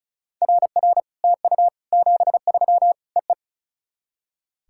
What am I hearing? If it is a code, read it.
Morse code, heard as a keyed tone, about 34 wpm, 700 Hz. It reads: RR TU 73 EE
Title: SOTA QSO Practice 35wpm